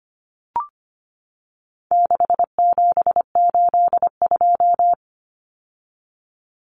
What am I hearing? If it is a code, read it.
Morse code, heard as a keyed tone, about 25 wpm, 700 Hz. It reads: 6782
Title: Common Cut Numbers - 4-Digit 25wpm